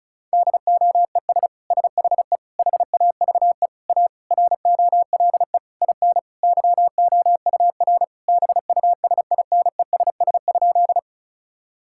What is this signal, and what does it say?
Morse code, heard as a keyed tone, about 36 wpm, 700 Hz. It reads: DOES SHE HAVE A ROLE IN YOUR BUSINESS?
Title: Sentences from Top 1000 Words - Review Part 6 - 35wpm